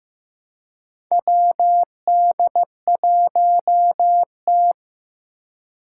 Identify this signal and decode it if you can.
Morse code, heard as a keyed tone, about 15 wpm, 700 Hz. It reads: WD1T
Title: Four Challenging Letter-Number Combinations 15wpm